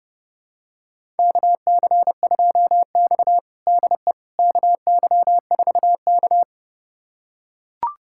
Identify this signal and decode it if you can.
Morse code, heard as a keyed tone, about 32 wpm, 700 Hz. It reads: KC2X DE KY4K
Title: USA Callsign DE Callsign - 4 Characters - 30wpm